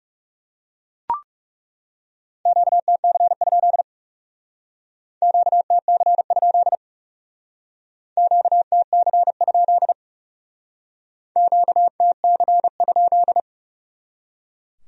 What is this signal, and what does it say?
Morse code, heard as a keyed tone, about 38 wpm, 700 Hz. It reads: QTC? QTC? QTC? QTC?